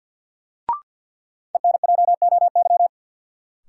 Morse code, heard as a keyed tone, about 53 wpm, 700 Hz. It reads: ENJOY
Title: Sentences from Top 1300 Words - New Word Drill Alt - 50wpm